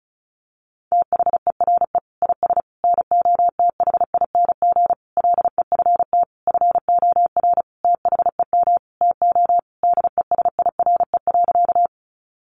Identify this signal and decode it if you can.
Morse code, heard as a keyed tone, about 34 wpm, 700 Hz. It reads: THERE IS NOTHING LEFT FOR THEM TO DESIRE.
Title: Sentences from Top 1100 Words 35wpm